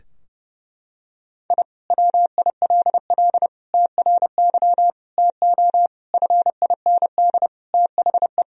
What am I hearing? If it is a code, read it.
Morse code, heard as a keyed tone, about 32 wpm, 700 Hz. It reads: I WILL TRY TO FIND THE
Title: Sentences from Top 1000 Words - Review Part 4 - 30wpm